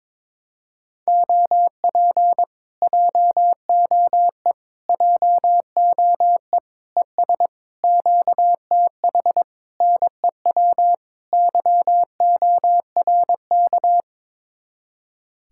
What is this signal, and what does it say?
Morse code, heard as a keyed tone, about 22 wpm, 700 Hz. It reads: OP JOE JOE ES QTH NEW YORK
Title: Speed-Racing Common QSO Phrases Encore - 22wpm